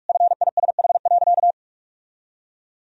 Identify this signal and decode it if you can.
Morse code, heard as a keyed tone, about 46 wpm, 700 Hz. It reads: FISH.
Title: Sentences from Top 1000 Words - Review Part 1 - 45wpm